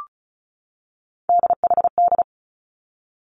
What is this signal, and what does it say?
Morse code, heard as a keyed tone, about 36 wpm, 700 Hz. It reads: DHD
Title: Three Challenging Letter-Number Combinations 35wpm